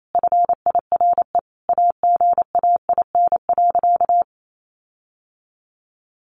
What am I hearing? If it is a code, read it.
Morse code, heard as a keyed tone, about 28 wpm, 700 Hz. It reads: FIRE AGAIN.
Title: Sentences from Top 1400 Words 28wpm